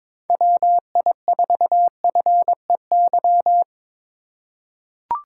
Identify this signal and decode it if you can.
Morse code, heard as a keyed tone, about 22 wpm, 700 Hz. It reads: WI4FEY
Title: USA Call Signs - 6 Characters - 6x Character Spacing - 22wpm